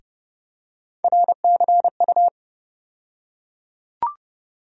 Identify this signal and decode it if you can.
Morse code, heard as a keyed tone, about 31 wpm, 700 Hz. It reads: RCU
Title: USA Call Sign Postfixes - 3 Letters 30wpm